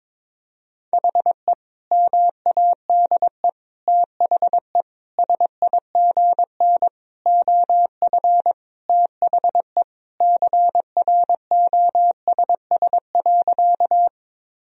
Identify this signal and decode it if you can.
Morse code, heard as a keyed tone, about 22 wpm, 700 Hz. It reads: HE MADE THE SIGN OF THE CROSS.
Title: Sentences from Top 1000 Words - Review Part 1 - 22wpm